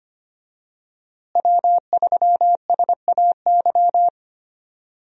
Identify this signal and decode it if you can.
Morse code, heard as a keyed tone, about 25 wpm, 700 Hz. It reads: W3SAY
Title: USA Call Signs - 5 Characters - Repeated 2x - 25wpm